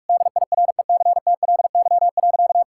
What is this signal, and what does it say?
Morse code, heard as a keyed tone, about 41 wpm, 700 Hz. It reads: DIRECTLY.
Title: Sentences from Top 1000 Words - Review Part 5 - 45wpm